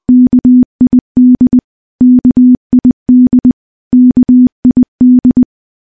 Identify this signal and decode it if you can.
Morse code, heard as a keyed tone, about 19 wpm, 250 Hz. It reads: KID KID KID
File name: morse / morse-training-20WPM